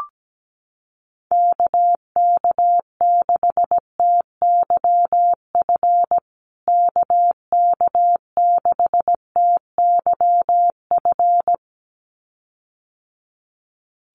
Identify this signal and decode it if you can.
Morse code, heard as a keyed tone, about 17 wpm, 700 Hz. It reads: KK6TYF KK6TYF
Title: USA Call Signs Repeated 2x 17wpm